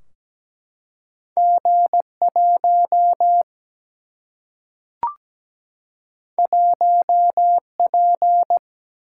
Morse code, heard as a keyed tone, about 17 wpm, 700 Hz. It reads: G 1 1 P
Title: Single-Letter Single-Number 17wpm